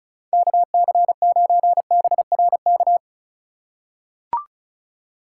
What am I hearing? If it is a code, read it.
Morse code, heard as a keyed tone, about 36 wpm, 700 Hz. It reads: KC9BRK
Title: USA Call Signs - 6 Characters - 2x Character Spacing - 35wpm